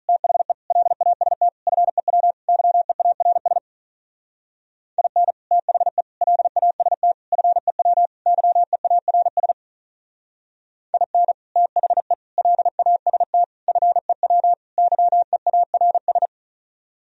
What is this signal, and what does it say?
Morse code, heard as a keyed tone, about 43 wpm, 700 Hz. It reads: THE LAST FEW YEARS IN THE LAST FEW YEARS IN THE LAST FEW YEARS
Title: Speed-Racing - Sets of 5 Words 35wpm